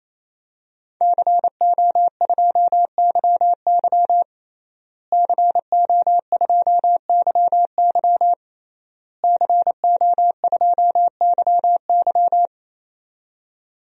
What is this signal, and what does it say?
Morse code, heard as a keyed tone, about 28 wpm, 700 Hz. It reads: CO2YY CO2YY CO2YY
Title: DXCC-100 Call Signs - 5 Characters - Repeated 3x - 28wpm